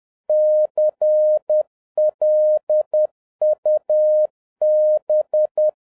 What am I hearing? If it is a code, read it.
Morse code, heard as a keyed tone, about 10 wpm, 610 Hz. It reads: CLUB
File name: 2011-07-mp3 / n46 JOHN K3TN 10wpm